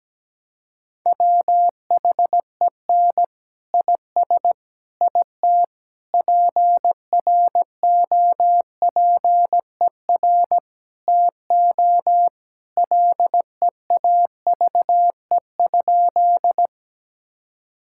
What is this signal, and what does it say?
Morse code, heard as a keyed tone, about 17 wpm, 700 Hz. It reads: WHEN IS IT PROPER TO LEAVE?
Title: Sentences from Top 1000 Words - Review Part 5 - 17wpm